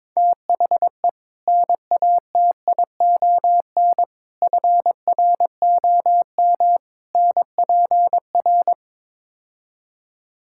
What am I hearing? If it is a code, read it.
Morse code, heard as a keyed tone, about 22 wpm, 700 Hz. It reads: THE NATION FROM NPR
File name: practice / 22wpm / Build A Phrase - 5-Word Phrases - 22wpm